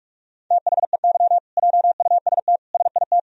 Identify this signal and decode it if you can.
Morse code, heard as a keyed tone, about 41 wpm, 700 Hz. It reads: THEY JUST SIT
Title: Sentences from Top 1000 Words - Review Part 3 - 45wpm